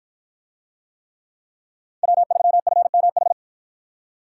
Morse code, heard as a keyed tone, about 51 wpm, 700 Hz. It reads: W3FMH